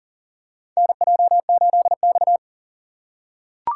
Morse code, heard as a keyed tone, about 40 wpm, 690 Hz. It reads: NJ8X
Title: USA Call Signs - 4x Character Spacing - 40wpm